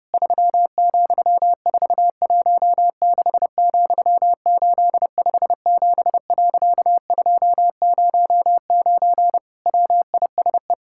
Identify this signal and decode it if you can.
Morse code, heard as a keyed tone, about 30 wpm, 700 Hz. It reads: 3,416,857.209 WISE
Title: Pangram Challenge v2 - 30wpm